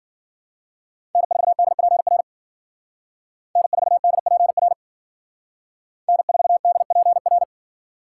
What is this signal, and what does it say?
Morse code, heard as a keyed tone, about 52 wpm, 700 Hz. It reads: N4DPR N4DPR N4DPR